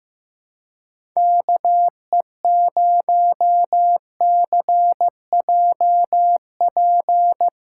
Morse code, heard as a keyed tone, about 15 wpm, 700 Hz. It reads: KE0CJP
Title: USA Callsign DE Callsign - 6 Characters - 15wpm